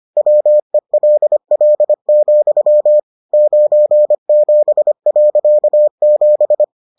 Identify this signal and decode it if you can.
Morse code, heard as a keyed tone, about 25 wpm, 600 Hz. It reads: WELL, 97.7